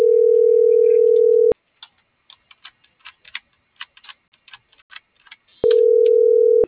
Ringback tone (North America), 440 + 480 Hz.